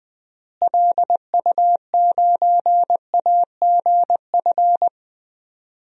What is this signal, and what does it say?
Morse code, heard as a keyed tone, about 20 wpm, 700 Hz. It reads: LU9AGF
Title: DXCC-100 Call Signs - 6 Characters - Repeated 2x - 20wpm